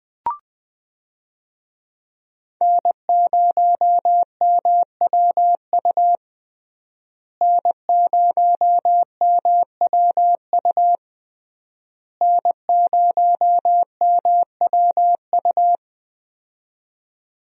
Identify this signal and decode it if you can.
Morse code, heard as a keyed tone, about 20 wpm, 700 Hz. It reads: N0MWU N0MWU N0MWU
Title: USA Call Signs - 5 Characters - Repeated 3x - 20wpm